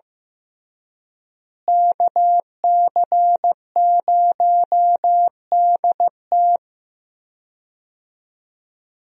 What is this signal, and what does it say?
Morse code, heard as a keyed tone, about 15 wpm, 700 Hz. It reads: KC0DT
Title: USA Call Signs - 5 Characters - Repeated 3x - 15wpm